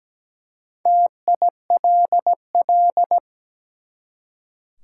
Morse code, heard as a keyed tone, about 17 wpm, 700 Hz. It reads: TILL
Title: Four Letter Words - 17wpm